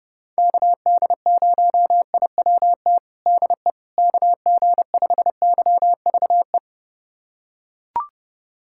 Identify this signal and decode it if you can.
Morse code, heard as a keyed tone, about 30 wpm, 700 Hz. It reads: KD0IWT DE KG5YVE
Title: USA Callsign DE Callsign - 6 Characters - 30wpm